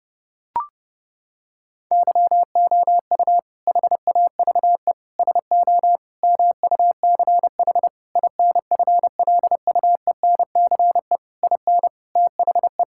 Morse code, heard as a keyed tone, about 32 wpm, 700 Hz. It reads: YOU HAVE SO MUCH INFLUENCE IN THE
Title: Sentences from Top 1600 Words 30wpm